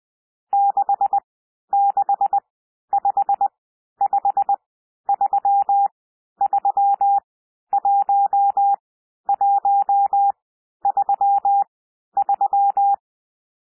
Morse code, heard as a keyed tone, about 20 wpm, spaced as for 11 wpm, 800 Hz. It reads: 6655331133
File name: SOUND NUMBERS 20 wpm